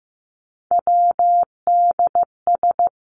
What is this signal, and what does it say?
Morse code, heard as a keyed tone, about 15 wpm, 700 Hz. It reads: WDS
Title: Abbreviations Repeated 3x - 15wpm